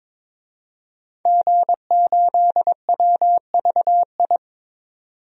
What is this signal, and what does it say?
Morse code, heard as a keyed tone, about 22 wpm, 700 Hz. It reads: G8WVI
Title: DXCC-100 Call Signs - 5 Characters - 22wpm